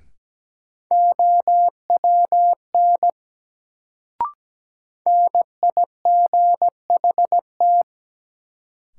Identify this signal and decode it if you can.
Morse code, heard as a keyed tone, about 17 wpm, 700 Hz. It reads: OWN NIGHT
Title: Sentences from Top 300 Words - New Word Drill - 17wpm